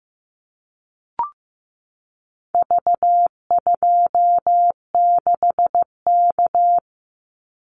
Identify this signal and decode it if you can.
Morse code, heard as a keyed tone, about 15 wpm, 700 Hz. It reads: V26K
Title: Call Signs Repeated 2x - 15wpm